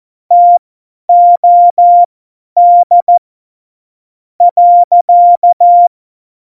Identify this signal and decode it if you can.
Morse code, heard as a keyed tone, about 14 wpm, spaced as for 10 wpm, 700 Hz. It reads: TOD .
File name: Morsekurs / Witze / Witz 21. August 2018, 50 F70